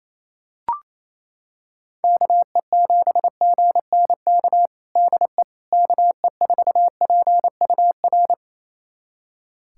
Morse code, heard as a keyed tone, about 28 wpm, 700 Hz. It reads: KE7GNK DE KE4PUR
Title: USA Callsign DE Callsign - 6 Characters - 28wpm